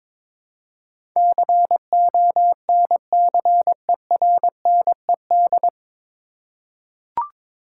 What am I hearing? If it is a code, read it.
Morse code, heard as a keyed tone, about 22 wpm, 700 Hz. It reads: CONCERNED
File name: 9 Letter Words - First 3 Letters Given - 22wpm